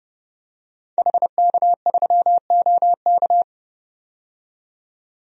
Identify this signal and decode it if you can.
Morse code, heard as a keyed tone, about 32 wpm, 700 Hz. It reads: HK3OK
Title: DXCC-100 Call Signs - 5 Characters - 30wpm